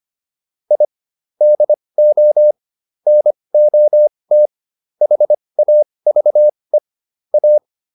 Morse code, heard as a keyed tone, about 25 wpm, spaced as for 20 wpm, 600 Hz. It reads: I DO NOT HAVE A